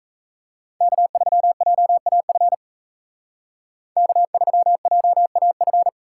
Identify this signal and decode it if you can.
Morse code, heard as a keyed tone, about 33 wpm, 700 Hz. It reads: K3JAF K3JAF
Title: Speed Racing - USA Call Signs 28wpm